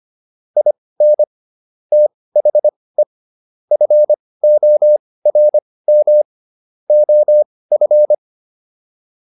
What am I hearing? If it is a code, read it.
Morse code, heard as a keyed tone, about 25 wpm, spaced as for 18 wpm, 600 Hz. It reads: IN THE FORM OF